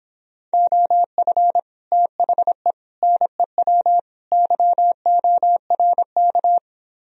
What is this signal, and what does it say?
Morse code, heard as a keyed tone, about 26 wpm, 700 Hz. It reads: OF THE NEW YORK